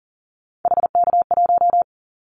Morse code, heard as a keyed tone, about 42 wpm, 700 Hz. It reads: HK1
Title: Callsign Prefixes for Top 100 DXCC - 40wpm